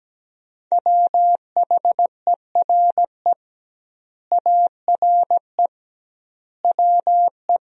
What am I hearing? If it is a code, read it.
Morse code, heard as a keyed tone, about 17 wpm, 700 Hz. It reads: WHERE ARE WE
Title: 2x Word-Spacing - Sentences from Top 200 Words - 17wpm